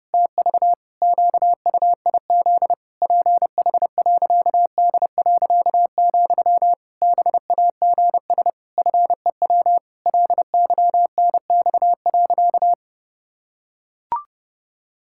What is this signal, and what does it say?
Morse code, heard as a keyed tone, about 32 wpm, 700 Hz. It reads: TV QUIZ PH.D., BAGS FEW LYNX.
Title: Pangram Challenge v2 - 30wpm